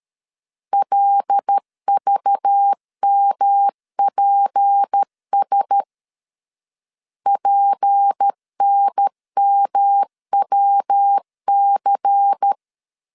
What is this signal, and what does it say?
Morse code, heard as a keyed tone, about 13 wpm, 800 Hz. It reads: LVMPS PNMWC
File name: LEZIONI / 6 PKX0 / 6 P 12wpm 2033